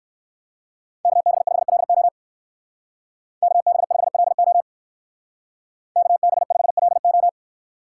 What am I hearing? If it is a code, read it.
Morse code, heard as a keyed tone, about 55 wpm, 700 Hz. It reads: KB5LO KB5LO KB5LO
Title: Speed Racing - USA Call Signs 45wpm